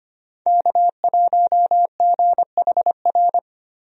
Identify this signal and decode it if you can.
Morse code, heard as a keyed tone, about 25 wpm, 700 Hz. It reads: K1GHR
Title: USA Call Signs - 5 Characters - 25wpm